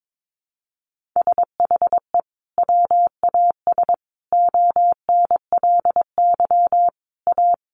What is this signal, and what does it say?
Morse code, heard as a keyed tone, about 22 wpm, 700 Hz. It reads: SHE WAS ONLY A
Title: Sentences from Top 400 Words 22wpm